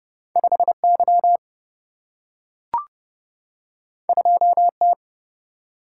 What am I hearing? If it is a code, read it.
Morse code, heard as a keyed tone, about 30 wpm, 700 Hz. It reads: 5Y 2T
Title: Single-Letter Single-Number 30wpm